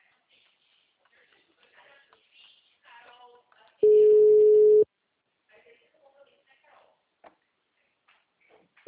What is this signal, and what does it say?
Ringback tone (Europe, 425 Hz), with speech.